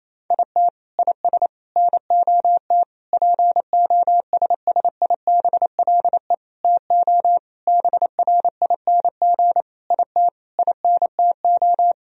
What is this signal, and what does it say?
Morse code, heard as a keyed tone, about 28 wpm, 700 Hz. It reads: IT IS NOT POSSIBLE TO BRING IT INTO
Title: Sentences from Top 1000 Words - Review Part 4 - 28wpm